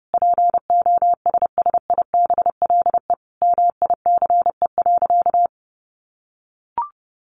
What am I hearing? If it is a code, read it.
Morse code, heard as a keyed tone, about 28 wpm, 700 Hz. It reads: POSSIBLE MICE.
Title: A Sound Of Thunder 30wpm